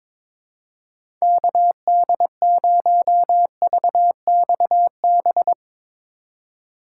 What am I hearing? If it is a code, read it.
Morse code, heard as a keyed tone, about 22 wpm, 700 Hz. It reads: KD0VXB
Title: USA Call Signs - 6 Characters - 2x Character Spacing - 22wpm